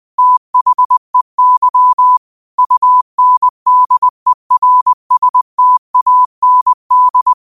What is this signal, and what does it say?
Morse code, heard as a keyed tone, about 20 wpm, 1000 Hz. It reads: THEY UNDERSTAND
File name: qotdfiles / 20wpm